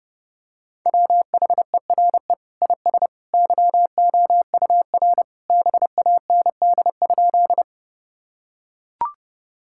Morse code, heard as a keyed tone, about 32 wpm, 700 Hz. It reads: WHERE IS YOUR BAND?
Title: Sentences from Top 1600 Words 30wpm